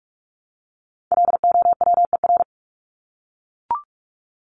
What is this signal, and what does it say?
Morse code, heard as a keyed tone, about 44 wpm, 700 Hz. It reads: LOWER
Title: Top 501-750 Words 45wpm